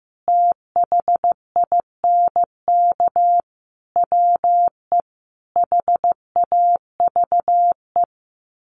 Morse code, heard as a keyed tone, about 15 wpm, 700 Hz. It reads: THINK WE HAVE
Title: Sets of 4 Words 15wpm